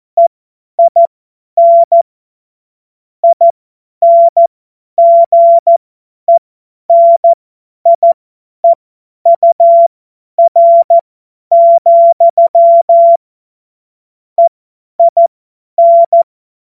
Morse code, heard as a keyed tone, about 14 wpm, spaced as for 10 wpm, 680 Hz. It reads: EIN INGENIEUR, EIN
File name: Morsekurs / Witze / Witz 21. Juni 2022 50 F70